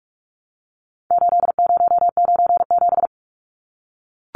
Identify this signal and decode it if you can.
Morse code, heard as a keyed tone, about 44 wpm, 700 Hz. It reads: 8097
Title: Numbers - 4-Digit 45wpm